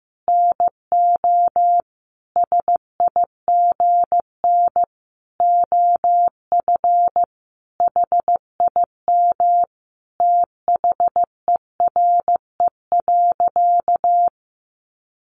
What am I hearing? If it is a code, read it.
Morse code, heard as a keyed tone, about 15 wpm, 700 Hz. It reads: NO SIGN OF HIM THERE.
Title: Sentences from Top 1000 Words 15wpm